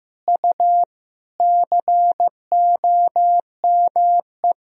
Morse code, heard as a keyed tone, about 15 wpm, 700 Hz. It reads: U COME
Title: Sentences from Top 200 Words and Abbreviations 15wpm